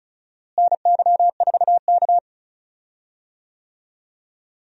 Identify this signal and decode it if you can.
Morse code, heard as a keyed tone, about 34 wpm, 700 Hz. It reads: NY4K
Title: USA Call Signs - 4 Characters - Repeated 2x - 35wpm